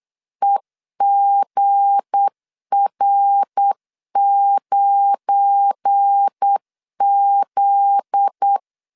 Morse code, heard as a keyed tone, about 8 wpm, 800 Hz. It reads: EGR9Z
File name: LEZIONI / 8 YJ45 / 8 YJ45 8wpm 3214